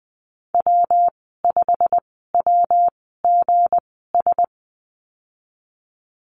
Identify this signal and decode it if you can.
Morse code, heard as a keyed tone, about 20 wpm, spaced as for 14 wpm, 700 Hz. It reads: W5WGS
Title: 167 Lesson 27g - Number 7 - Call Signs 20wpm